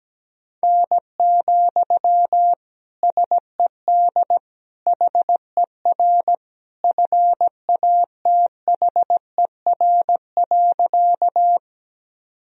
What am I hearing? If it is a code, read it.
Morse code, heard as a keyed tone, about 17 wpm, 700 Hz. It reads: N, SED HER FATHER.
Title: Sentences from Top 300 Words and Abbreviations 17wpm